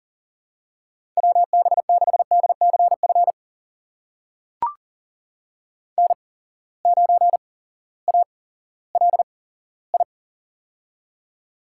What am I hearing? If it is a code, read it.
Morse code, heard as a keyed tone, about 40 wpm, spaced as for 18 wpm, 700 Hz. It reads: WB6DCF N9ALI
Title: USA Call Signs - 8x Character Spacing - 40wpm